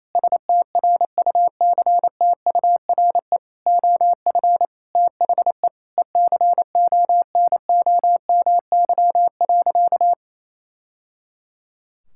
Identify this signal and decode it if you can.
Morse code, heard as a keyed tone, about 28 wpm, 700 Hz. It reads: STRUCTURE OF THE ECONOMY.
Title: Sentences from Top 1000 Words - Review Part 6 - 28wpm